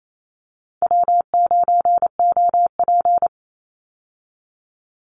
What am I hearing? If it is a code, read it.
Morse code, heard as a keyed tone, about 28 wpm, 700 Hz. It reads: W9OP
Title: Call Signs Repeated 2x - 28wpm